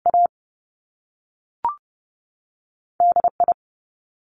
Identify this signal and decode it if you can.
Morse code, heard as a keyed tone, about 30 wpm, 700 Hz. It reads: A D I
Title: Top Two Letter Combinations in Words 30wpm